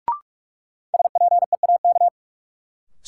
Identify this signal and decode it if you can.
Morse code, heard as a keyed tone, about 45 wpm, 700 Hz. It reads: SPEAK
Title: Sentences from Top 1000 Words - Review Part 2 - New Word Drill - 45wpm